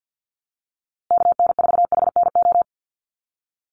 Morse code, heard as a keyed tone, about 50 wpm, 700 Hz. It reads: KN4HNO
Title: USA Call Signs Repeated 3x 50wpm